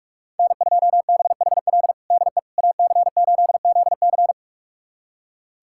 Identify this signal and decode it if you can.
Morse code, heard as a keyed tone, about 46 wpm, 700 Hz. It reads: N2BHL DE AC8ZC
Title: USA Callsign DE Callsign - 5 Characters - 45wpm